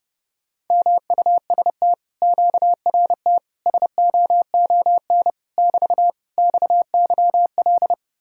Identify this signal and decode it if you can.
Morse code, heard as a keyed tone, about 33 wpm, 700 Hz. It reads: MUST QRT SOON <BT> XYL